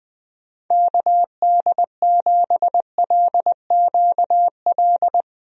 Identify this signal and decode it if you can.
Morse code, heard as a keyed tone, about 20 wpm, 700 Hz. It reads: KD7LQL
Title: USA Call Signs - 6 Characters - Repeated 2x - 20wpm